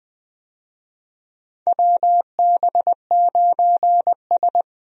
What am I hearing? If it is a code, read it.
Morse code, heard as a keyed tone, about 20 wpm, 700 Hz. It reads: WB9S
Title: USA Call Signs - 4 Characters - Repeated 2x - 20wpm